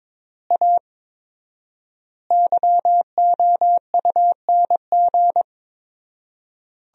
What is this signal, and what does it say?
Morse code, heard as a keyed tone, about 22 wpm, 700 Hz. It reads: A YOUNG
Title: 4x Word-Spacing - Sentences from Top 400 Words - 22wpm